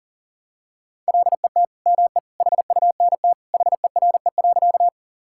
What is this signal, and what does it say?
Morse code, heard as a keyed tone, about 40 wpm, 700 Hz. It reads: LET ME HUNT HERE.
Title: Sentences from Top 1600 Words 40wpm